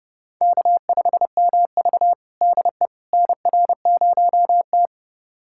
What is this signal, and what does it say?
Morse code, heard as a keyed tone, about 32 wpm, 700 Hz. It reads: K5MV DE NR0T
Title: USA Callsign DE Callsign - 4 Characters - 30wpm